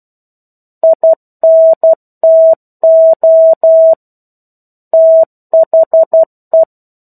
Morse code, heard as a keyed tone, about 12 wpm, 650 Hz. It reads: INTO THE
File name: 01-Chapter-01 12 WPM